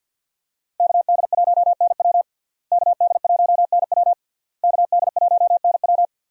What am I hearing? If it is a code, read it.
Morse code, heard as a keyed tone, about 47 wpm, 690 Hz. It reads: KD1NW KD1NW KD1NW
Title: USA Call Signs - 5 Characters - Repeated 3x - 50wpm